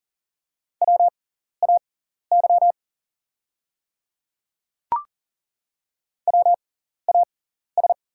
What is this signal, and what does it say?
Morse code, heard as a keyed tone, about 40 wpm, spaced as for 14 wpm, 700 Hz. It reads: WAY WAS
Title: 6x Character Spacing - Top 100 Words - 40wpm